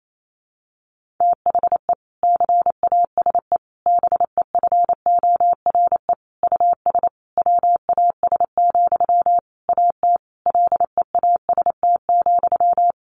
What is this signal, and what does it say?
Morse code, heard as a keyed tone, about 28 wpm, 700 Hz. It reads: THE CASE BEFORE US WAS, AT LEAST,
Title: Sentences from Top 1400 Words 28wpm